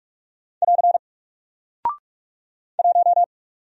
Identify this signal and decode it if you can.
Morse code, heard as a keyed tone, about 44 wpm, 700 Hz. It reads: <AR> 1
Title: Single Character (Letter:Number:Punctuation:Prosign) 45wpm